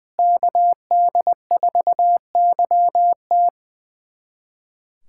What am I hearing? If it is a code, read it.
Morse code, heard as a keyed tone, about 20 wpm, 700 Hz. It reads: KD4YT
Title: USA Callsign DE Callsign - 5 Characters - 20wpm